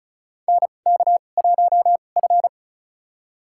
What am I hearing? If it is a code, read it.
Morse code, heard as a keyed tone, about 36 wpm, spaced as for 25 wpm, 700 Hz. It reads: NK1F
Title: USA Call Signs - 4 Characters - 2x Character Spacing - 35wpm